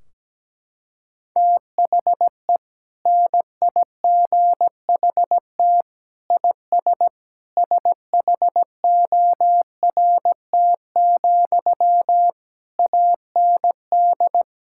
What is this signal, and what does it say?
Morse code, heard as a keyed tone, about 17 wpm, 700 Hz. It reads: THE NIGHT IS SHORT, AND
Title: Sentences from Top 1600 Words 17wpm